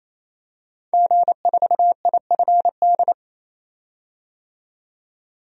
Morse code, heard as a keyed tone, about 27 wpm, 700 Hz. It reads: G4IFD
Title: DXCC-100 Call Signs - 5 Characters - 28wpm